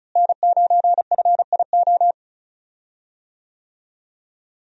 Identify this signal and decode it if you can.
Morse code, heard as a keyed tone, about 36 wpm, 700 Hz. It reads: N9FIO
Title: USA Call Signs - 5 Characters - Repeated 3x - 35wpm